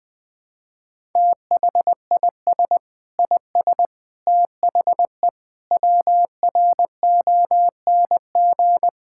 Morse code, heard as a keyed tone, about 20 wpm, 700 Hz. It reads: THIS IS THE WRONG
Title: Sentences from Top 1000 Words - Review Part 6 - 20wpm